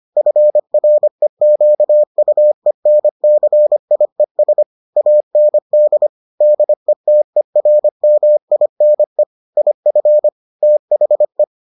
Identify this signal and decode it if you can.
Morse code, heard as a keyed tone, about 25 wpm, 600 Hz. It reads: FREQUENCIES AND DETERMINE IF THE